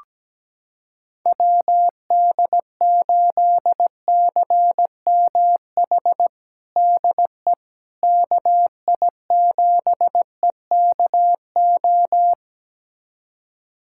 Morse code, heard as a keyed tone, about 17 wpm, 700 Hz. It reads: WD8CMH DE KI7EKO
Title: USA Callsign DE Callsign 17wpm